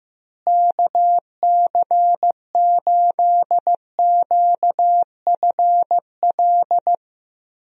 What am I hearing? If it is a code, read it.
Morse code, heard as a keyed tone, about 15 wpm, 700 Hz. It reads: KC8QFL
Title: USA Call Signs - 6 Characters - Repeated 3x - 15wpm